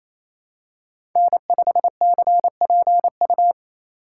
Morse code, heard as a keyed tone, about 28 wpm, 700 Hz. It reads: N5CPU
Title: USA Call Signs - 5 Characters - 4x Character Spacing - 28wpm